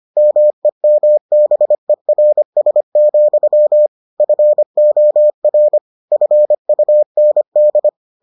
Morse code, heard as a keyed tone, about 25 wpm, 600 Hz. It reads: MEMBERS, FOR FUND